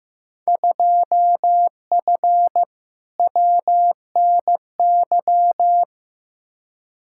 Morse code, heard as a keyed tone, about 15 wpm, 700 Hz. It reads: 2F WNY
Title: Field Day Exchanges Encore - 15wpm